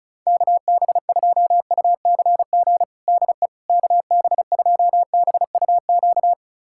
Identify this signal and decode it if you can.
Morse code, heard as a keyed tone, about 34 wpm, 700 Hz. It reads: KB2UCG DE KB2BUQ
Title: USA Callsign DE Callsign - 6 Characters - 35wpm